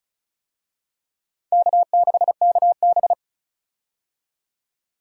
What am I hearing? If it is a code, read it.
Morse code, heard as a keyed tone, about 34 wpm, 700 Hz. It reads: K6KB
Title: USA Call Signs - 4 Characters - 35wpm